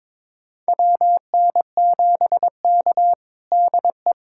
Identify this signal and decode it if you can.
Morse code, heard as a keyed tone, about 22 wpm, 700 Hz. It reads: WN7K DE
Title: USA Callsign DE Callsign - 4 Characters - 22wpm